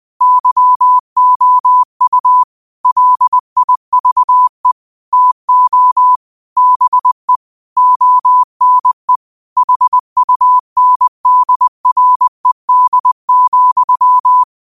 Morse code, heard as a keyed tone, about 20 wpm, 1000 Hz. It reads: YOU LIVE TO BE ONE HUNDRED,
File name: qotdfiles / 20wpm